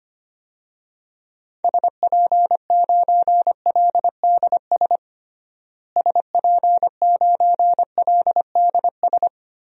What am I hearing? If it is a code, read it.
Morse code, heard as a keyed tone, about 25 wpm, 700 Hz. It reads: SP9LDS SP9LDS
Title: DXCC-100 Call Signs - 6 Characters - Repeated 2x - 25wpm